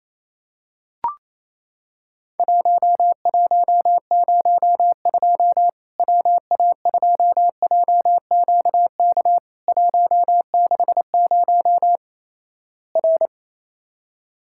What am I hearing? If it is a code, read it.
Morse code, heard as a keyed tone, about 28 wpm, 700 Hz. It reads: 1102 WA2JQK 160 R
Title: Worked All Europe DX Contest - QTC Exchange - 28wpm